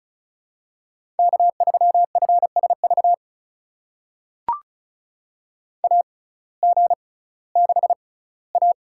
Morse code, heard as a keyed tone, about 36 wpm, spaced as for 19 wpm, 700 Hz. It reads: K3FSV AG6A
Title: USA Call Signs - 6x Character Spacing - 35wpm